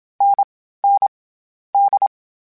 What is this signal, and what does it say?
Morse code, heard as a keyed tone, about 27 wpm, 800 Hz. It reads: N N D
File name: Difficult Combos / Difficult Combos-mp3s / NDB NDB6 @25